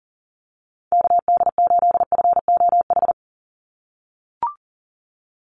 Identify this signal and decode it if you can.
Morse code, heard as a keyed tone, about 40 wpm, 690 Hz. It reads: KD8FOH
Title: USA Call Signs 40wpm